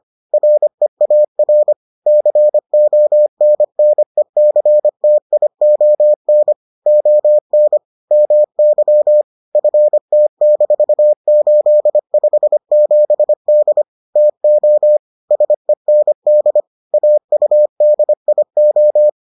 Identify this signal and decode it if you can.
Morse code, heard as a keyed tone, about 25 wpm, 600 Hz. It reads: REAR CONNECTION ON MY FT-857D TO SEND AUDIO